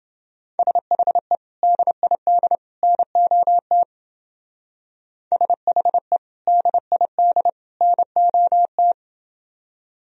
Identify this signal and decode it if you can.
Morse code, heard as a keyed tone, about 29 wpm, 700 Hz. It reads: SHE DID NOT SHE DID NOT